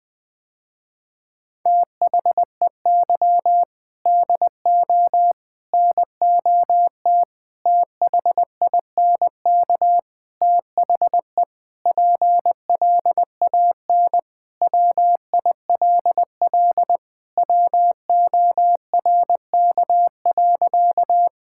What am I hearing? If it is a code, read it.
Morse code, heard as a keyed tone, about 21 wpm, 700 Hz. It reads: THEY DO NOT THINK THE PLAN WILL WORK.
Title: Sentences with 4-Word Phrases - Spoken Beginning - 20wpm